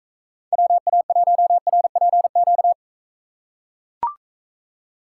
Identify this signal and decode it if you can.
Morse code, heard as a keyed tone, about 44 wpm, 700 Hz. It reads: WA1RPQ